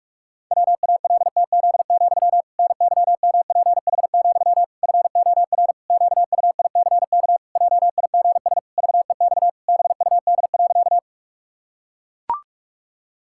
Morse code, heard as a keyed tone, about 44 wpm, 700 Hz. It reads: WALTZ, NYMPH, FOR QUICK JIGS VEX BUD.
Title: Pangram Challenge v2 - 45wpm